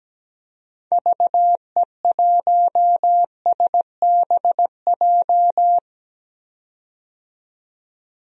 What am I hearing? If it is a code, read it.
Morse code, heard as a keyed tone, about 17 wpm, 700 Hz. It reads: VE1SBJ
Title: DXCC-100 Call Signs - 6 Characters - 17wpm